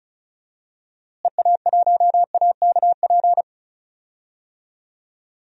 Morse code, heard as a keyed tone, about 36 wpm, 700 Hz. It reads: EA1AKP
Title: DXCC-100 Call Signs - 6 Characters - Repeated 3x - 35wpm